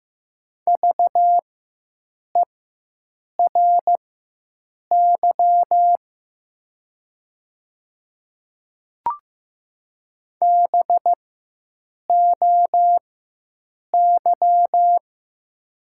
Morse code, heard as a keyed tone, about 15 wpm, spaced as for 7 wpm, 690 Hz. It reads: VERY BOY
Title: 4x Character Spacing - Top 200 Words - 15wpm